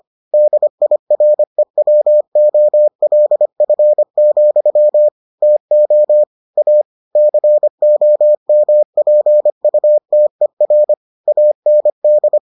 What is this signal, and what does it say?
Morse code, heard as a keyed tone, about 25 wpm, 600 Hz. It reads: DIREWOLF, TO A COMPUTER AND